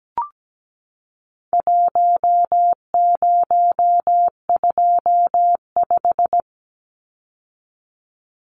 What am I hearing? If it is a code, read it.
Morse code, heard as a keyed tone, about 17 wpm, 700 Hz. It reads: 1025
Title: Numbers - Contest Style - 4-Digit 17wpm